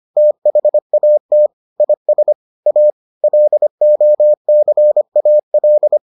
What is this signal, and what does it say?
Morse code, heard as a keyed tone, about 25 wpm, 600 Hz. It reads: THAT IS A LOCAL